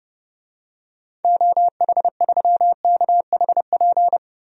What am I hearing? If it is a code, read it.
Morse code, heard as a keyed tone, about 32 wpm, 700 Hz. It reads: OH3KHP
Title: DXCC-100 Call Signs - 6 Characters - 30wpm